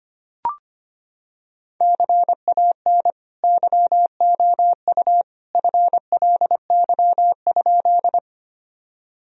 Morse code, heard as a keyed tone, about 25 wpm, 700 Hz. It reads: CAN YOU FLY?
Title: Sentences from Top 1000 Words - Review Part 2 - 25wpm